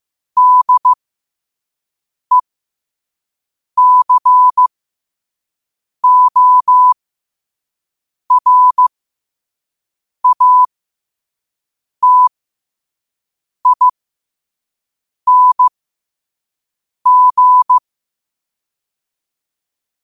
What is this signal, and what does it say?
Morse code, heard as a keyed tone, about 15 wpm, spaced as for 5 wpm, 1000 Hz. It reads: DECORATING
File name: qotdfiles / 5wpm